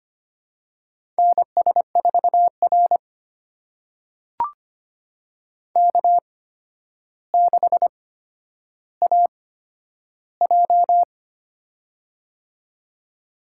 Morse code, heard as a keyed tone, about 25 wpm, spaced as for 11 wpm, 700 Hz. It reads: NS4R K6AJ
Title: USA Call Signs - 4 Characters - 8x Character Spacing - 25wpm